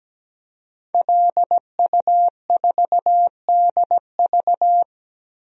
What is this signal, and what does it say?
Morse code, heard as a keyed tone, about 17 wpm, 700 Hz. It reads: LU4DV
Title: DXCC-100 Call Signs - 5 Characters - Repeated 2x - 17wpm